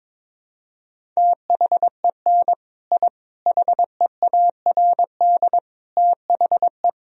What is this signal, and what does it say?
Morse code, heard as a keyed tone, about 22 wpm, 700 Hz. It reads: THEN I HEARD THE
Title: Sentences from Top 1000 Words - Review Part 1 - 22wpm